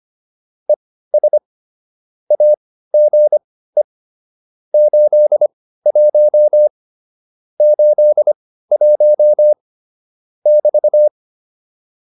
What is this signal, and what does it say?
Morse code, heard as a keyed tone, about 25 wpm, spaced as for 15 wpm, 600 Hz. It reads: ES AGE 81 81 <BT>